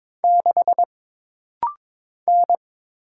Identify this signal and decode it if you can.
Morse code, heard as a keyed tone, about 22 wpm, 700 Hz. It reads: 6 N
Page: This rapid-fire timing is geared to push your ICR (Instant Character Recognition) skills.